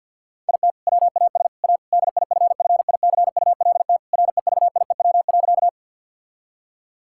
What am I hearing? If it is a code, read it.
Morse code, heard as a keyed tone, about 50 wpm, 700 Hz. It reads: IT WAS A DIFFICULT REVIEW.
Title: Sentences from Top 1000 Words - Review Part 6 - 50wpm